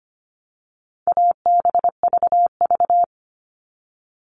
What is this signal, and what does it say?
Morse code, heard as a keyed tone, about 25 wpm, 700 Hz. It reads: ABVV
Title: All Cut Numbers - 4-Digit 25wpm